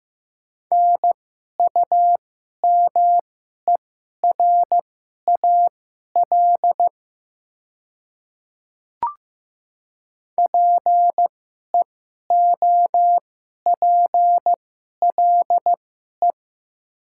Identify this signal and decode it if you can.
Morse code, heard as a keyed tone, about 15 wpm, spaced as for 11 wpm, 700 Hz. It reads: NUMERAL PEOPLE
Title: 2x Character Spacing - Top 500 Words - 15wpm